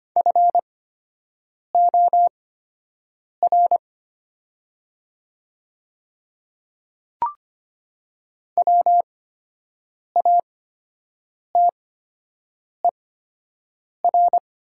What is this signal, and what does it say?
Morse code, heard as a keyed tone, about 24 wpm, spaced as for 7 wpm, 690 Hz. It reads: FOR WATER
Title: 8x Character Spacing - Top 200 Words - 25wpm